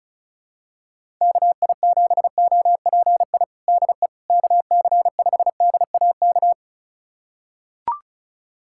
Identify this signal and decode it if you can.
Morse code, heard as a keyed tone, about 36 wpm, 700 Hz. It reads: KI7OPI DE KC5DAK
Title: USA Callsign DE Callsign 35wpm